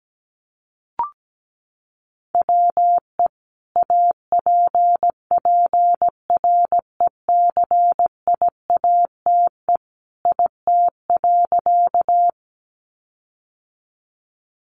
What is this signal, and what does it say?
Morse code, heard as a keyed tone, about 17 wpm, 700 Hz. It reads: WE APPRECIATE IT.
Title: Sentences from Top 1300 Words 17wpm